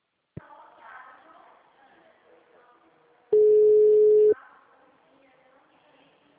Ringback tone (Europe, 425 Hz).